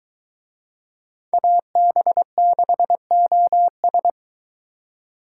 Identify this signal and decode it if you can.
Morse code, heard as a keyed tone, about 21 wpm, 700 Hz. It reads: AB6OS